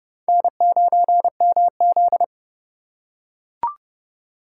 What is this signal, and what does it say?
Morse code, heard as a keyed tone, about 32 wpm, 700 Hz. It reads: N9MZ
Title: USA Call Signs - 4 Characters - 2x Character Spacing - 30wpm